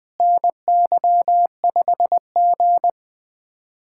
Morse code, heard as a keyed tone, about 20 wpm, 700 Hz. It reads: NY5G
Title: USA Callsign DE Callsign - 4 Characters - 20wpm